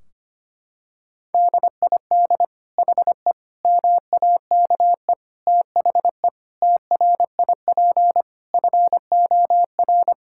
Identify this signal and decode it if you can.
Morse code, heard as a keyed tone, about 25 wpm, 700 Hz. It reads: DID HE MAKE THE TRIP FOR
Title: Sentences from Top 1600 Words 25wpm